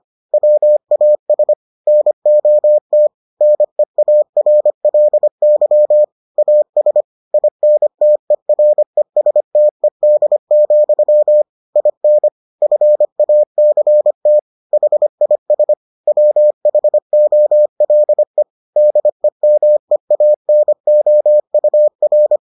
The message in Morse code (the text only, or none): WAS NOT NEARLY AS INTERESTED, IN FACT HIS WHOLE DEMEANOUR